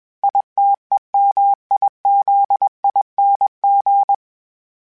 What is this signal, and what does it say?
Morse code, heard as a keyed tone, about 21 wpm, 800 Hz. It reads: ITEMIZING
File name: Special INGs / Special INGs-mp3s / Sngl Ing @20